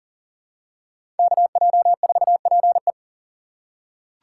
Morse code, heard as a keyed tone, about 40 wpm, 700 Hz. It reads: KJ4PE
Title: USA Call Signs - 5 Characters - 2x Character Spacing - 40wpm